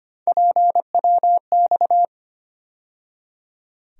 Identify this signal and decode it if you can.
Morse code, heard as a keyed tone, about 25 wpm, 700 Hz. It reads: PWX
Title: USA Call Sign Postfixes - 3 Letters 25wpm